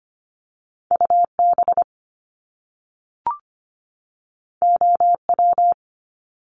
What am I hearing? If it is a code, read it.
Morse code, heard as a keyed tone, about 25 wpm, 700 Hz. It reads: UB OW
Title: Top Two Letter Combinations in Words 25wpm